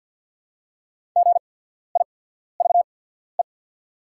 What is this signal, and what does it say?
Morse code, heard as a keyed tone, about 48 wpm, spaced as for 14 wpm, 690 Hz. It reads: GIVE
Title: 8x Character Spacing - Top 500 Words - 50wpm